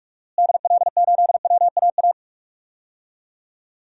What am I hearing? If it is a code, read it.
Morse code, heard as a keyed tone, about 41 wpm, 700 Hz. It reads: DL8WAA
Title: DXCC-100 Call Signs - 6 Characters - Repeated 2x - 45wpm